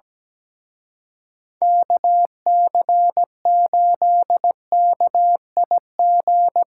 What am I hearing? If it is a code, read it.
Morse code, heard as a keyed tone, about 19 wpm, 700 Hz. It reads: KC8KIG
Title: Speed Racing - USA 2x3 Call Signs 17wpm